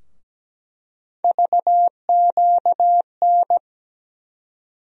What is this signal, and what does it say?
Morse code, heard as a keyed tone, about 17 wpm, 700 Hz. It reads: VQN
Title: USA Call Sign Postfixes - 3 Letters 17wpm